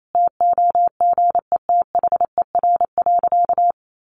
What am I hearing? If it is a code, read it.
Morse code, heard as a keyed tone, about 28 wpm, 700 Hz. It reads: TOGETHER.
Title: Sentences from Top 1300 Words 28wpm